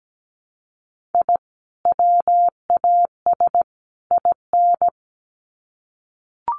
Morse code, heard as a keyed tone, about 17 wpm, 700 Hz. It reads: I WAS IN
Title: Sets of 3 Words 17wpm